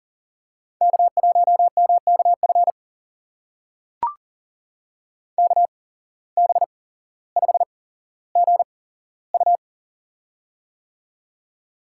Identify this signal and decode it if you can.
Morse code, heard as a keyed tone, about 40 wpm, spaced as for 17 wpm, 700 Hz. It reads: K1MKF KB5GU
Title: USA Call Signs - 5 Characters - 8x Character Spacing - 40wpm